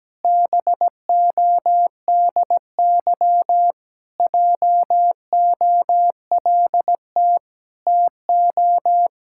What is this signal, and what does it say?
Morse code, heard as a keyed tone, about 17 wpm, 700 Hz. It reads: BODY JOLT TO
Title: Pangram Challenge v2 - 17wpm